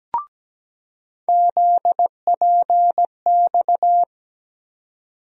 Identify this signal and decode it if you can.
Morse code, heard as a keyed tone, about 17 wpm, 700 Hz. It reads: ZPX
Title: USA Call Sign Postfixes - 3 Letters 17wpm